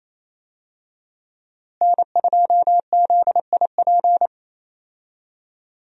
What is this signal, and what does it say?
Morse code, heard as a keyed tone, about 28 wpm, 700 Hz. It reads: N2ZIP
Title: USA Call Signs - 5 Characters - Repeated 3x - 28wpm